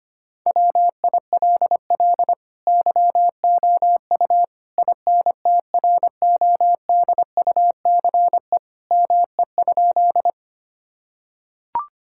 Morse code, heard as a keyed tone, about 25 wpm, 700 Hz. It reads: WILL YOU INTRODUCE ME?
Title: Sentences from Top 1600 Words 25wpm